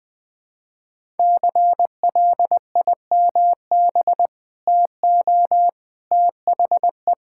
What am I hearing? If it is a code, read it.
Morse code, heard as a keyed tone, about 20 wpm, 700 Hz. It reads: CLIMB TO THE
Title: Sentences with 5-Word Phrases - Spoken Beginning - 20wpm